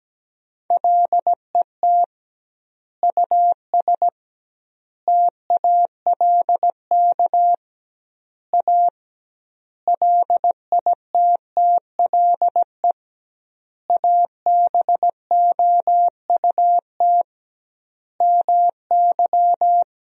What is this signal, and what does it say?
Morse code, heard as a keyed tone, about 17 wpm, 700 Hz. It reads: LET US TALK A LITTLE ABOUT MY
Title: 2x Word-Spacing - Sentences from Top 400 Words - 17wpm